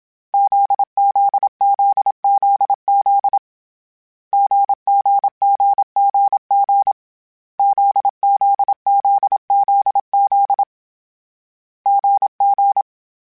Morse code, heard as a keyed tone, about 27 wpm, 800 Hz. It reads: ZZZZZ GGGGG ZZZZZ GG